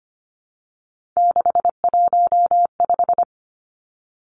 Morse code, heard as a keyed tone, about 25 wpm, 700 Hz. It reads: 615
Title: Three Challenging Letter-Number Combinations 25wpm